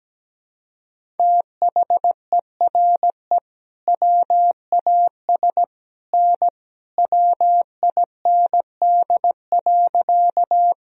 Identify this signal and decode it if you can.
Morse code, heard as a keyed tone, about 17 wpm, 700 Hz. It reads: THERE WAS N WIND.
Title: Sentences from Top 400 Words and Abbreviations 17wpm